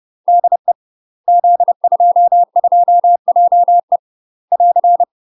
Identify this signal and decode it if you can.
Morse code, heard as a keyed tone, about 30 wpm, 700 Hz. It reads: DE Z22JE <AR>
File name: UK MASTER QSOs TRAINING / 30WPM / UK MASTER QSOs 0024